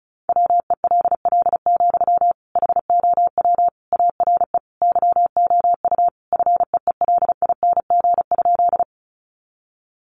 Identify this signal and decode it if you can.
Morse code, heard as a keyed tone, about 34 wpm, 700 Hz. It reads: WELL, HOW ARE YOU FEELING?
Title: Sentences from Top 1200 Words 35wpm